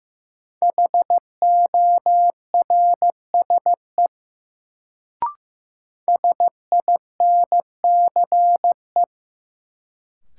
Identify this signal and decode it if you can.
Morse code, heard as a keyed tone, about 15 wpm, 700 Hz. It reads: HORSE SINCE
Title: Sentences from Top 300 Words - New Word Drill - 15wpm